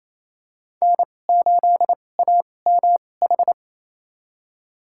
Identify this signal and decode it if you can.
Morse code, heard as a keyed tone, about 27 wpm, spaced as for 20 wpm, 700 Hz. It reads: N8AMH
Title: USA Call Signs - 5 Characters - 2x Character Spacing - 28wpm